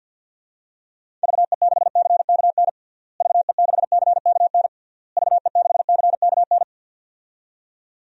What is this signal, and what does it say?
Morse code, heard as a keyed tone, about 53 wpm, 700 Hz. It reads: VE6CKN VE6CKN VE6CKN
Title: DXCC-100 Call Signs - 6 Characters - Repeated 3x - 50wpm